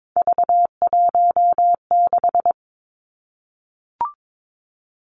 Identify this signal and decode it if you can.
Morse code, heard as a keyed tone, about 22 wpm, 700 Hz. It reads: V16
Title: Three Challenging Letter-Number Combinations 22wpm